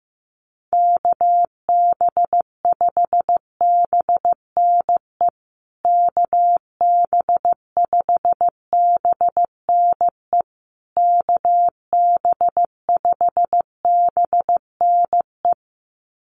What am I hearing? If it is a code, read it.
Morse code, heard as a keyed tone, about 15 wpm, 700 Hz. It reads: KB5BNE KB5BNE KB5BNE
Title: USA Call Signs Repeated 3x 15wpm